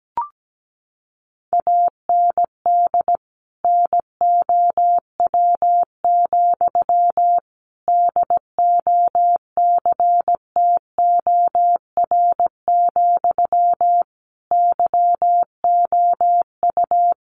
Morse code, heard as a keyed tone, about 17 wpm, 700 Hz. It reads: AND NOW, DOCTOR, YOU
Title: Sentences from Top 1500 Words 17wpm